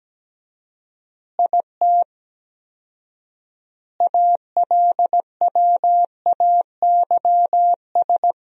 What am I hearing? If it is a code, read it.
Morse code, heard as a keyed tone, about 17 wpm, 700 Hz. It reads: IT ALWAYS
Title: 4x Word-Spacing - Sentences from Top 300 Words - 17wpm